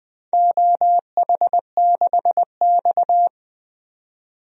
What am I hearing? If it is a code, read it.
Morse code, heard as a keyed tone, about 20 wpm, 700 Hz. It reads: OH6X
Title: DXCC-100 Call Signs - 4 Characters - 20wpm